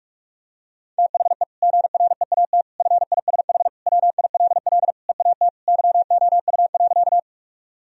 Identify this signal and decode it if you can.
Morse code, heard as a keyed tone, about 46 wpm, 700 Hz. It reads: THE GREAT FISH WILL EAT YOU.
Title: Sentences from Top 1000 Words - Review Part 1 - 45wpm